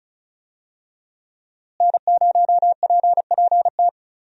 Morse code, heard as a keyed tone, about 36 wpm, 700 Hz. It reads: N0PPT
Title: USA Call Signs - 5 Characters - Repeated 3x - 35wpm